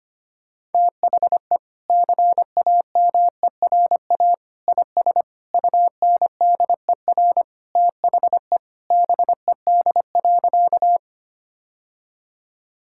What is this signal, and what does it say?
Morse code, heard as a keyed tone, about 25 wpm, 700 Hz. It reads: THE CAMERA IS UNDER THE BED.
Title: Sentences from Top 1000 Words - Review Part 5 - 25wpm